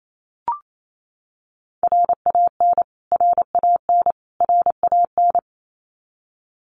Morse code, heard as a keyed tone, about 28 wpm, 700 Hz. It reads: RAN RAN RAN
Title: Top 500 Words Repeated 28wpm